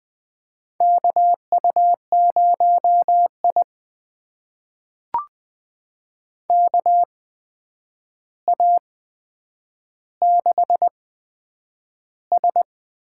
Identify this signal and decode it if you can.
Morse code, heard as a keyed tone, about 20 wpm, spaced as for 9 wpm, 700 Hz. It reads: KU0I KA6S
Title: USA Call Signs - 4 Characters - 8x Character Spacing - 20wpm